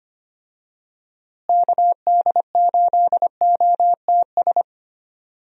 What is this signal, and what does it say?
Morse code, heard as a keyed tone, about 25 wpm, 700 Hz. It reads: KD8OTS
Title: USA Call Signs - 6 Characters - Repeated 3x - 25wpm